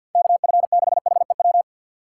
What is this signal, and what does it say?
Morse code, heard as a keyed tone, about 52 wpm, 700 Hz. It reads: KF6HEW
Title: USA Call Signs - 6 Characters - 3x Character Spacing - 50wpm